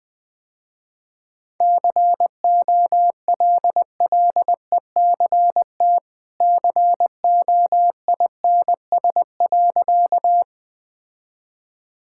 Morse code, heard as a keyed tone, about 21 wpm, 700 Hz. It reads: COLLECT COINS.
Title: Easy Sentences with Hints - 20wpm